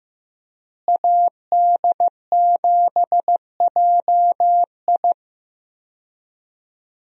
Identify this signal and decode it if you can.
Morse code, heard as a keyed tone, about 15 wpm, 700 Hz. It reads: AD7JI
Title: USA Call Signs - 5 Characters - Repeated 2x - 15wpm